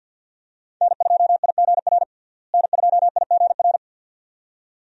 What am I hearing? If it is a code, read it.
Morse code, heard as a keyed tone, about 49 wpm, 700 Hz. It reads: N2IGR N2IGR
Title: USA Call Signs - 5 Characters - Repeated 2x - 50wpm